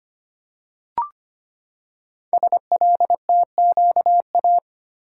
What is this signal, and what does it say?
Morse code, heard as a keyed tone, about 25 wpm, 700 Hz. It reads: SLTQA
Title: Code Groups - 5 Letters - 25wpm